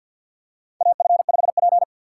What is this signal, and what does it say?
Morse code, heard as a keyed tone, about 52 wpm, 700 Hz. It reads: AF5P
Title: K1USN SST QSO Practice - 50wpm